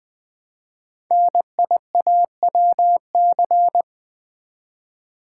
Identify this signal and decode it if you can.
Morse code, heard as a keyed tone, about 20 wpm, 700 Hz. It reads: NIAWC
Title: Code Groups - 5 Letters - 20wpm